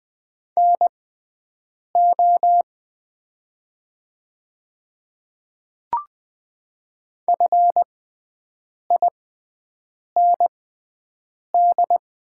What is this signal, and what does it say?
Morse code, heard as a keyed tone, about 20 wpm, spaced as for 7 wpm, 700 Hz. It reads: NO FIND
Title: 6x Character Spacing - Top 100 Words - 20wpm